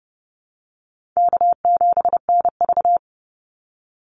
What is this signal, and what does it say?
Morse code, heard as a keyed tone, about 30 wpm, 700 Hz. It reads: K7NV
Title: Easy Call Signs - 30wpm